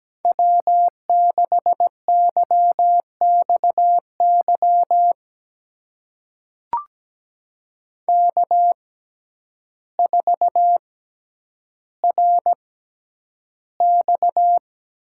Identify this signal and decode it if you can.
Morse code, heard as a keyed tone, about 17 wpm, spaced as for 9 wpm, 700 Hz. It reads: W6YXY K4RX
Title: USA Call Signs - 5 Characters - 6x Character Spacing - 17wpm